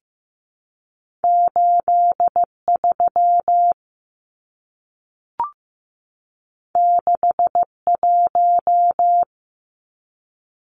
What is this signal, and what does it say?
Morse code, heard as a keyed tone, about 15 wpm, 700 Hz. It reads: 83 61
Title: Numbers - 2-Digit 15wpm